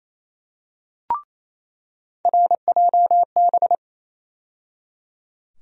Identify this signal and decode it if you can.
Morse code, heard as a keyed tone, about 28 wpm, 700 Hz. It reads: RJB
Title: USA Call Sign Postfixes - 3 Letters 28wpm